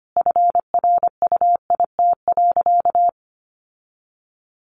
Morse code, heard as a keyed tone, about 25 wpm, 700 Hz. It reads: FRUIT.
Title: Sentences from Top 1300 Words 25wpm